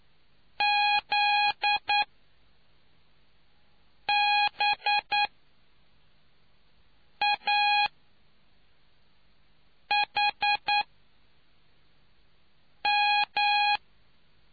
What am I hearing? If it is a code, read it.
Morse code, heard as a keyed tone, about 9 wpm, spaced as for 4 wpm, 830 Hz. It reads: ZBAHM